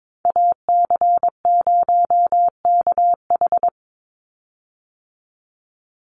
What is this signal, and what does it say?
Morse code, heard as a keyed tone, about 22 wpm, 700 Hz. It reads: AC0KH
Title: USA Call Signs 22wpm